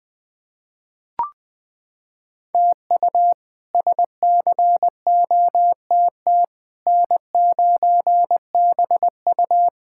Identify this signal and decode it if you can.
Morse code, heard as a keyed tone, about 20 wpm, 700 Hz. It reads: TU SCOTT N9BU
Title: K1USN SST QSO Practice - 20wpm